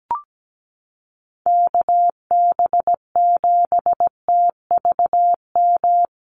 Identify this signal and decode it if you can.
Morse code, heard as a keyed tone, about 17 wpm, 700 Hz. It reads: KB7TVM
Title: USA Call Signs Repeated 3x 17wpm